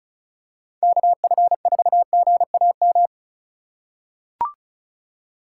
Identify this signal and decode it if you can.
Morse code, heard as a keyed tone, about 36 wpm, 700 Hz. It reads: KF4GAM
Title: USA Call Signs - 6 Characters - 3x Character Spacing - 35wpm